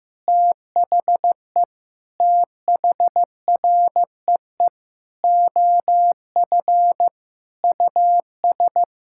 Morse code, heard as a keyed tone, about 15 wpm, 700 Hz. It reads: THE THREE OF US
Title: Sentences from Top 1000 Words - Review Part 6 - 15wpm